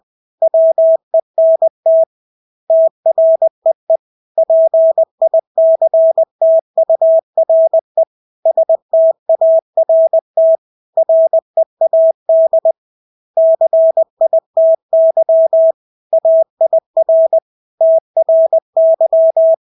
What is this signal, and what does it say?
Morse code, heard as a keyed tone, about 20 wpm, 650 Hz. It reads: WENT TREE PICTURE START READ CITY AIR TRY
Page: Engelske ord 20 wpm